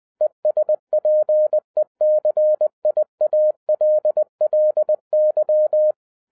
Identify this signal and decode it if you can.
Morse code, heard as a keyed tone, about 20 wpm, 610 Hz. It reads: ESPECIALLY
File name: 2012-01-mp3 / n16-THROUGH EIGHT MONTHS-20wpm